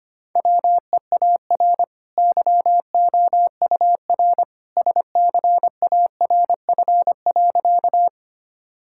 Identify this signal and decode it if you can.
Morse code, heard as a keyed tone, about 25 wpm, 700 Hz. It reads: WEAR YOUR SCARF.
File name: Easy Sentences with Hints - 25wpm